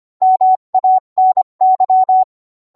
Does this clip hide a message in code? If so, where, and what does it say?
Yes: Morse code from 0.2 to 2.2 s, heard as a keyed tone, about 25 wpm, 750 Hz. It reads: MANY